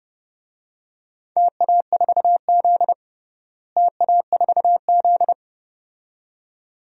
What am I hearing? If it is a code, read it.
Morse code, heard as a keyed tone, about 32 wpm, 700 Hz. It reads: TA4Z TA4Z
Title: DXCC-100 Call Signs - 4 Characters - Repeated 2x - 30wpm